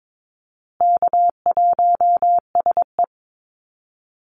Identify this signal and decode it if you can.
Morse code, heard as a keyed tone, about 22 wpm, 700 Hz. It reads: K1SE
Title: Call Signs Repeated 2x - 22wpm